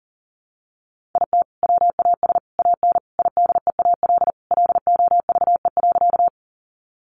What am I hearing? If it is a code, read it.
Morse code, heard as a keyed tone, about 40 wpm, 700 Hz. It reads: IT WAS AN IDEAL LOVE.
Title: Sentences from Top 1400 Words 40wpm